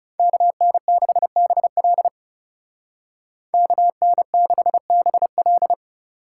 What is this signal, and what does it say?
Morse code, heard as a keyed tone, about 37 wpm, 700 Hz. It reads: KN6BL KN6BL
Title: Speed Racing - USA Call Signs 30wpm